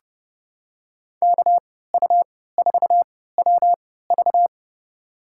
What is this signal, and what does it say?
Morse code, heard as a keyed tone, about 30 wpm, spaced as for 17 wpm, 700 Hz. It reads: KU4WV
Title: USA Call Signs - 5 Characters - 3x Character Spacing - 30wpm